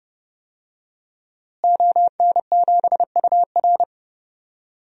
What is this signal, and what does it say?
Morse code, heard as a keyed tone, about 30 wpm, 700 Hz. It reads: ON7UR
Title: DXCC-100 Call Signs - 5 Characters - Repeated 3x - 30wpm